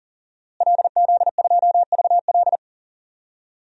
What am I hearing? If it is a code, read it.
Morse code, heard as a keyed tone, about 40 wpm, 700 Hz. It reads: LZ2VL
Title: DXCC-100 Call Signs - 5 Characters - Repeated 3x - 40wpm